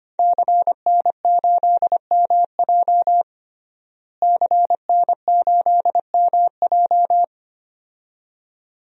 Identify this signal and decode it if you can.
Morse code, heard as a keyed tone, about 25 wpm, 700 Hz. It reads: CN8MJ CN8MJ
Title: DXCC-100 Call Signs - 5 Characters - Repeated 2x - 25wpm